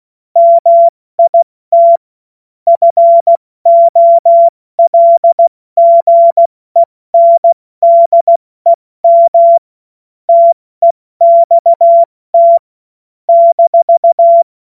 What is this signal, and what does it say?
Morse code, heard as a keyed tone, about 16 wpm, 680 Hz. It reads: MIT FOLGENDEM TEXT -
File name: Morsekurs / Witze / Witz 04. Oktober 2022 70 F80